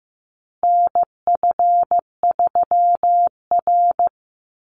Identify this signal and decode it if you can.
Morse code, heard as a keyed tone, about 15 wpm, 700 Hz. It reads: NF3R
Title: Call Signs Repeated 2x - 15wpm